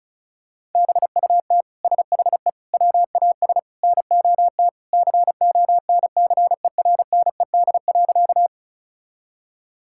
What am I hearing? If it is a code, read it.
Morse code, heard as a keyed tone, about 36 wpm, 700 Hz. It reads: BUT SHE WAS NOT CONCERNED.
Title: Sentences from Top 1000 Words - Review Part 6 - 35wpm